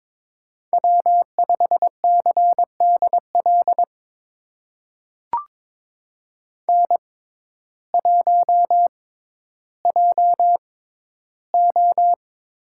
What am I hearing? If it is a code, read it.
Morse code, heard as a keyed tone, about 22 wpm, spaced as for 12 wpm, 700 Hz. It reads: W5CDL N1JO
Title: USA Call Signs - 6x Character Spacing - 22wpm